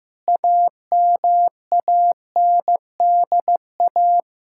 Morse code, heard as a keyed tone, about 15 wpm, 700 Hz. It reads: AMANDA
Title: K1USN SST QSO Practice 15wpm